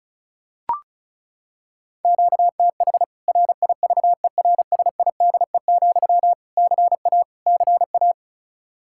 Morse code, heard as a keyed tone, about 34 wpm, 700 Hz. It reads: QTH RIVERSIDE, CA CA
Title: K1USN SST QSO Practice 35wpm